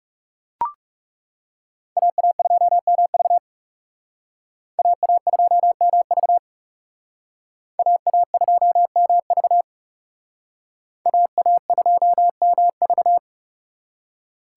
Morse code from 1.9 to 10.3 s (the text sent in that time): AA2MV AA2MV AA2MV